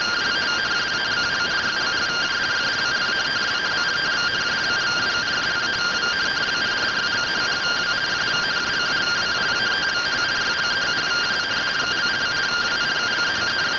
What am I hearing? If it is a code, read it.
Two-tone teleprinter signal, about 170 Hz shift.
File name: findings / MR AXIAL 71mm 3mm.gen-03 rtty rx